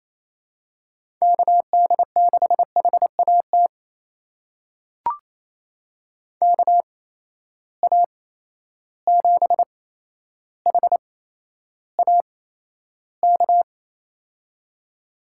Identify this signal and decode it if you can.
Morse code, heard as a keyed tone, about 28 wpm, spaced as for 12 wpm, 700 Hz. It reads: KD6HAT KA7HAK
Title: USA Call Signs - 6 Characters - 8x Character Spacing - 28wpm